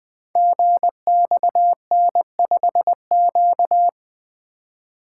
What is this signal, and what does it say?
Morse code, heard as a keyed tone, about 20 wpm, 700 Hz. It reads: GXN5Q
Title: Code Groups - 5 Letter:Numbers - 20wpm